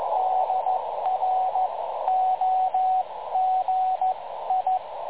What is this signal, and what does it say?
Morse code, heard as a keyed tone, about 15 wpm, spaced as for 13 wpm, 750 Hz, with radio noise. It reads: DROGI
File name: dlazaawansowanych / nagraniazpasma / websdr recording 3565.0kHz